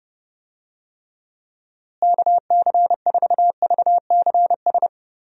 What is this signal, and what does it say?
Morse code, heard as a keyed tone, about 30 wpm, 700 Hz. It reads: KC4VCS
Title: USA Call Signs - 6 Characters - Repeated 2x - 30wpm